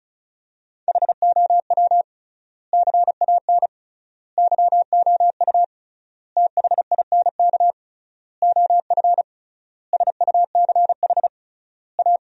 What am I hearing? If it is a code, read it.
Morse code, heard as a keyed tone, about 34 wpm, 700 Hz. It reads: HOW CAN YOU THINK OF SUCH A
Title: 3x Word-Spacing - Sentences from Top 200 Words - 35wpm